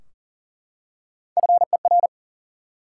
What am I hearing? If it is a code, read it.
Morse code, heard as a keyed tone, about 40 wpm, 700 Hz. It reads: FER
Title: Top 90 QSO Elements 40wpm